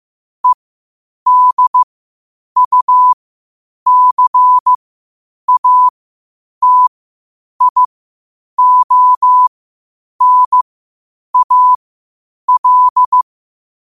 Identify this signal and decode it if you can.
Morse code, heard as a keyed tone, about 15 wpm, spaced as for 8 wpm, 1000 Hz. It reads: EDUCATIONAL
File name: 8wpm